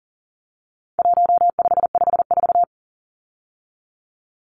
Morse code, heard as a keyed tone, about 37 wpm, 700 Hz. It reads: 1554
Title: Numbers - 4-Digit 40wpm